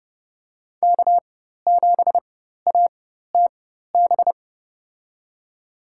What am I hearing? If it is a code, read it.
Morse code, heard as a keyed tone, about 30 wpm, spaced as for 14 wpm, 700 Hz. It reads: K7ATB
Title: USA Call Signs - 5 Characters - 4x Character Spacing - 30wpm